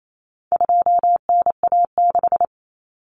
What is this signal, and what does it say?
Morse code, heard as a keyed tone, about 28 wpm, 700 Hz. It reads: 2NA6
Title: Common Cut Numbers - 4-Digit 28wpm